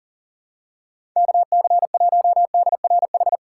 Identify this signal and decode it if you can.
Morse code, heard as a keyed tone, about 37 wpm, 700 Hz. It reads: KC1DRH
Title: USA Call Signs - 6 Characters - 2x Character Spacing - 40wpm